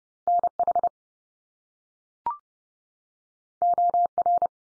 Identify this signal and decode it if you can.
Morse code, heard as a keyed tone, about 30 wpm, 700 Hz. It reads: NH OR
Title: US States 30WPM